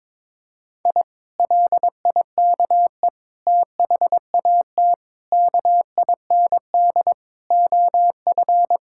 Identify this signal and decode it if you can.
Morse code, heard as a keyed tone, about 22 wpm, 700 Hz. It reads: I LIKE THAT KIND OF
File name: practice / 22wpm / Sentences with 3-Word Phrases - 22wpm